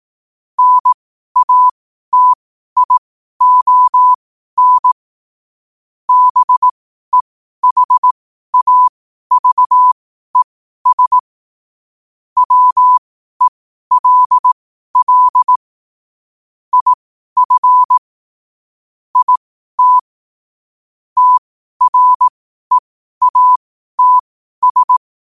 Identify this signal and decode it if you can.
Morse code, heard as a keyed tone, about 18 wpm, spaced as for 12 wpm, 1000 Hz. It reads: NATION BEHAVES WELL IF IT TREATS
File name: qotdfiles / 12wpm